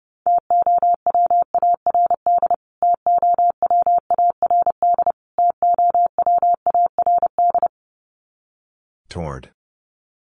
Morse code, heard as a keyed tone, about 30 wpm, 700 Hz. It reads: TOWARD TOWARD TOWARD
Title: Top 500 Words Repeated 30wpm